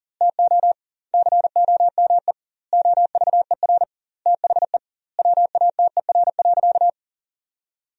Morse code, heard as a keyed tone, about 40 wpm, 700 Hz. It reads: TO COME OVER THE WATER.
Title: 2x Word-Spacing - Sentences from Top 300 Words - 40wpm